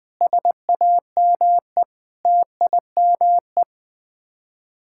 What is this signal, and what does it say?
Morse code, heard as a keyed tone, about 21 wpm, 700 Hz. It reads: SAME TIME
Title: Sets of 8 Words - 20wpm